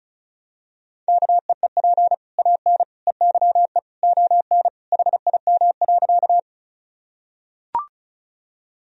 Morse code, heard as a keyed tone, about 36 wpm, 700 Hz. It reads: KEEP AN EYE ON HIM.
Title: Sentences from Top 1000 Words - Review Part 1 - 35wpm